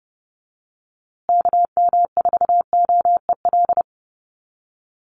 Morse code, heard as a keyed tone, about 30 wpm, 700 Hz. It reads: KM4OEL
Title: USA Call Signs Repeated 2x 30wpm